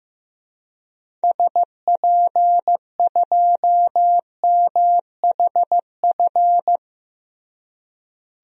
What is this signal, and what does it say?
Morse code, heard as a keyed tone, about 15 wpm, 700 Hz. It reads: SP2MHF
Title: DXCC-100 Call Signs - 6 Characters - Repeated 2x - 15wpm